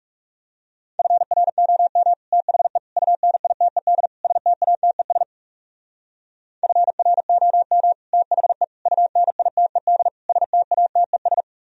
Morse code, heard as a keyed tone, about 38 wpm, 700 Hz. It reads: FROM THE UNITED STATES FROM THE UNITED STATES
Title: Speed-Racing - Sets of 4 Words 30wpm